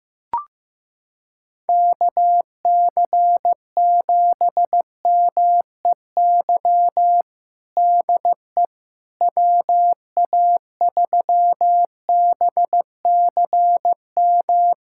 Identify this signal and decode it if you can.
Morse code, heard as a keyed tone, about 15 wpm, 700 Hz. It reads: KC7MEY DE WA3BCM
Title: USA Callsign DE Callsign 15wpm